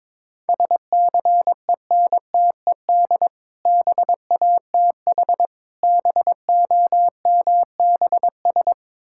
Morse code, heard as a keyed tone, about 22 wpm, 700 Hz. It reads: SCENTED BATH BOMBS
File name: Difficult Words with Hints - 22wpm